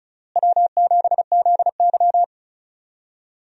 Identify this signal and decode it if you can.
Morse code, heard as a keyed tone, about 36 wpm, 700 Hz. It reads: W7ZY
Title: USA Call Signs - 4 Characters - Repeated 3x - 35wpm